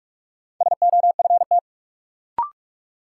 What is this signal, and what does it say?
Morse code, heard as a keyed tone, about 46 wpm, 700 Hz. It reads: SOFT
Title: Sentences from Top 1500 Words - New Word Drill Alt - 45wpm